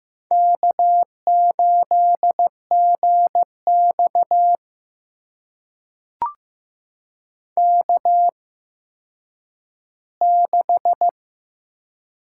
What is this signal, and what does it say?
Morse code, heard as a keyed tone, about 15 wpm, spaced as for 9 wpm, 700 Hz. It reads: K8GX K6
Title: USA Call Signs - 4 Characters - 8x Character Spacing - 15wpm